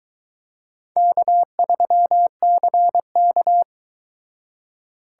Morse code, heard as a keyed tone, about 22 wpm, 700 Hz. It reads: K3CK